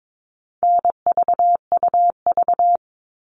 Morse code, heard as a keyed tone, about 22 wpm, 700 Hz. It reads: NVUV
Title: All Cut Numbers - 4-Digit 22wpm